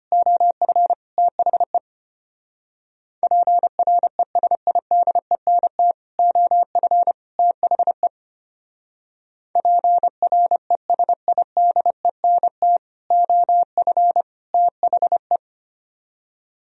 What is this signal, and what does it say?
Morse code, heard as a keyed tone, about 31 wpm, 700 Hz. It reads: OF THE PRESIDENT OF THE PRESIDENT OF THE
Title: Speed-Racing - Sets of 3 Words 25wpm